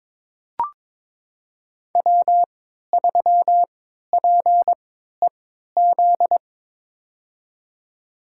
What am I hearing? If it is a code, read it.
Morse code, heard as a keyed tone, about 22 wpm, spaced as for 13 wpm, 700 Hz. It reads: W3PEZ
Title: USA Call Signs - 5 Characters - 3x Character Spacing - 22wpm